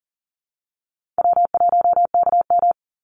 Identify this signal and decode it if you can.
Morse code, heard as a keyed tone, about 40 wpm, 700 Hz. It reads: W1KM
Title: Easy Call Signs - 40wpm